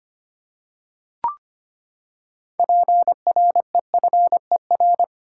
Morse code, heard as a keyed tone, about 25 wpm, 700 Hz. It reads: PREFER
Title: Six Letter Words - 25wpm